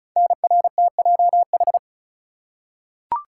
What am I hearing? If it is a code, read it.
Morse code, heard as a keyed tone, about 36 wpm, 690 Hz. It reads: NRTJH
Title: Code Groups - 5 Letters - 35wpm